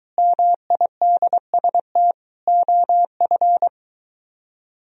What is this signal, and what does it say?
Morse code, heard as a keyed tone, about 23 wpm, 700 Hz. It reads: MIDST OF